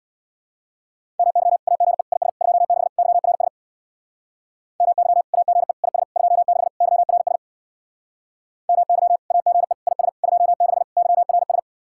Morse code, heard as a keyed tone, about 60 wpm, 700 Hz. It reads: MY AGE IS 26 YRS MY AGE IS 26 YRS MY AGE IS 26 YRS
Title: Speed-Racing Common QSO Phrases - 50wpm